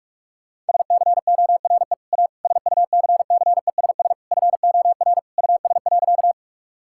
Morse code, heard as a keyed tone, about 42 wpm, 700 Hz. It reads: SCORE A SUCCESS FOR US.
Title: Sentences from Top 1600 Words 45wpm